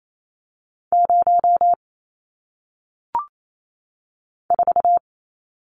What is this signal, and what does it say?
Morse code, heard as a keyed tone, about 28 wpm, 700 Hz. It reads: 0 4
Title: Numbers - 1-Digit 28wpm